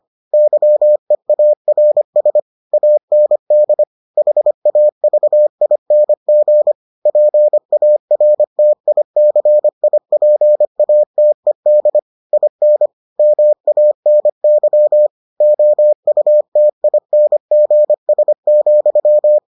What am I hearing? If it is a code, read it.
Morse code, heard as a keyed tone, about 25 wpm, 600 Hz. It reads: YEARS AND HAVING PARTICIPATED IN MANY OUTINGS,